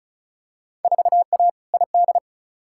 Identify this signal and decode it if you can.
Morse code, heard as a keyed tone, about 36 wpm, 700 Hz. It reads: 4A ID
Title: Field Day Exchanges - 35wpm